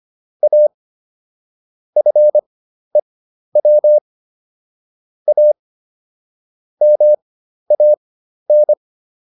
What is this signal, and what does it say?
Morse code, heard as a keyed tone, about 25 wpm, spaced as for 12 wpm, 600 Hz. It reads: A FEW A MAN